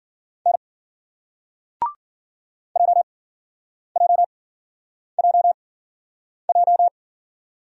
Morse code, heard as a keyed tone, about 50 wpm, spaced as for 9 wpm, 700 Hz. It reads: N JJJJ